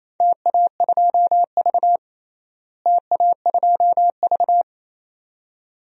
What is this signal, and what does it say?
Morse code, heard as a keyed tone, about 27 wpm, 700 Hz. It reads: TA2V TA2V
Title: DXCC-100 Call Signs - 4 Characters - Repeated 2x - 28wpm